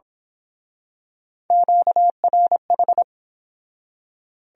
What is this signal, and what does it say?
Morse code, heard as a keyed tone, about 27 wpm, 700 Hz. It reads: QRH
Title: Speed-Racing Q-Codes - 22wpm